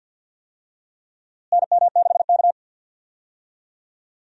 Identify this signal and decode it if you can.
Morse code, heard as a keyed tone, about 48 wpm, 700 Hz. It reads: NM6K
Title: USA Call Signs - 4 Characters - Repeated 2x - 50wpm